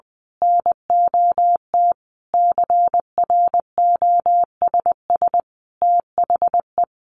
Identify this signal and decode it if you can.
Morse code, heard as a keyed tone, about 20 wpm, 700 Hz. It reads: NOT CROSS THE
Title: A Sound Of Thunder 20wpm